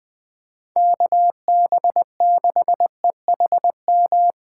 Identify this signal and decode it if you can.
Morse code, heard as a keyed tone, about 20 wpm, 700 Hz. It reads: KB6EHM
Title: Speed Racing - USA 2x3 Call Signs 17wpm